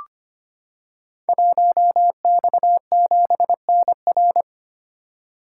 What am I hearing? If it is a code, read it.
Morse code, heard as a keyed tone, about 25 wpm, 700 Hz. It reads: 1X7NR
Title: Code Groups - 5 Letter:Numbers - 25wpm